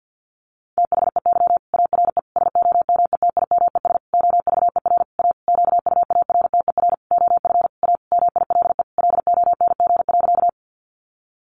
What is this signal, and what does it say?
Morse code, heard as a keyed tone, about 50 wpm, 700 Hz. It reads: THEY ARE SOMETIMES OVER A QUARTER OF A MILE LONG.
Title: Sentences from Top 1300 Words 50wpm